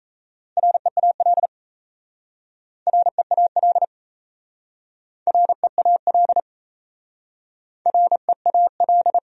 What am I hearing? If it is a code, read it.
Morse code, heard as a keyed tone, about 35 wpm, 700 Hz. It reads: REAL REAL REAL REAL
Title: Speed-Racing Top 500 Words 28wpm